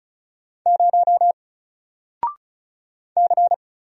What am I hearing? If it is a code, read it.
Morse code, heard as a keyed tone, about 36 wpm, 700 Hz. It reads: 0 C
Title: Single Letter-Number Alt 35wpm